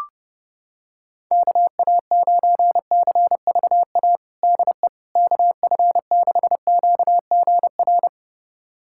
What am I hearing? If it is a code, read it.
Morse code, heard as a keyed tone, about 32 wpm, 700 Hz. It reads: KA9CVA DE KF6QGR
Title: USA Callsign DE Callsign 30wpm